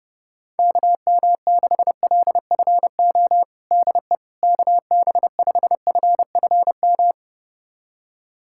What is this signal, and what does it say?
Morse code, heard as a keyed tone, about 30 wpm, 700 Hz. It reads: KM6LFO DE KB5FFM
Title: USA Callsign DE Callsign - 6 Characters - 30wpm